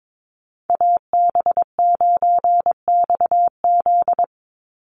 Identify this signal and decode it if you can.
Morse code, heard as a keyed tone, about 22 wpm, 700 Hz. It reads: AB9XZ
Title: USA Call Signs 22wpm